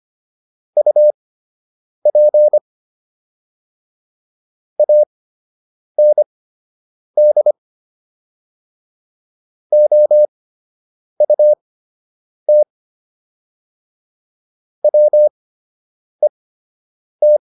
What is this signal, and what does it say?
Morse code, heard as a keyed tone, about 25 wpm, spaced as for 8 wpm, 600 Hz. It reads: UP AND OUT WET